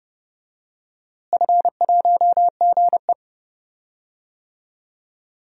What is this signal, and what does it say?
Morse code, heard as a keyed tone, about 30 wpm, 700 Hz. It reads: F1GE
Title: DXCC-100 Call Signs - 4 Characters - Repeated 2x - 30wpm